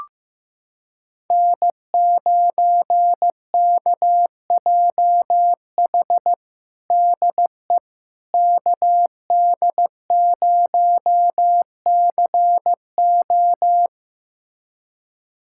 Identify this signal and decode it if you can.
Morse code, heard as a keyed tone, about 15 wpm, 700 Hz. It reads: N9KJH DE KD0CO
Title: USA Callsign DE Callsign - 5 Characters - 15wpm